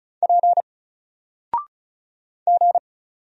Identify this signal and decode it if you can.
Morse code, heard as a keyed tone, about 35 wpm, 700 Hz. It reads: P G
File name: Single-Letter-Number-35wpm